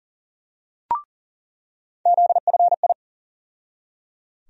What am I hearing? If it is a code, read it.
Morse code, heard as a keyed tone, about 37 wpm, 700 Hz. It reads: ZFI
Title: USA Call Sign Postfixes - 3 Letters 40wpm